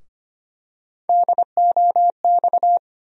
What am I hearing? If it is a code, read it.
Morse code, heard as a keyed tone, about 25 wpm, 700 Hz. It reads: DOX
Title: USA Call Sign Postfixes - 3 Letters 25wpm